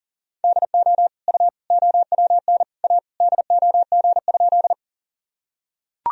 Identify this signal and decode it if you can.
Morse code, heard as a keyed tone, about 40 wpm, 700 Hz. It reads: DO U OWN A DOG?
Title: Sentences from Top 400 Words and Abbreviations 40wpm